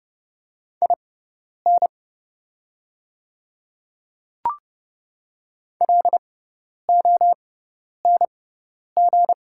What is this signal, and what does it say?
Morse code, heard as a keyed tone, about 30 wpm, spaced as for 10 wpm, 700 Hz. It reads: IN LONG
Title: 6x Character Spacing - Top 100 Words - 30wpm